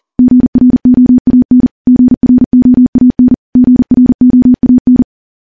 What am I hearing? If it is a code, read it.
Morse code, heard as a keyed tone, about 40 wpm, 250 Hz. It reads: GROAN GROAN GROAN
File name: morse / morse-training-40WPM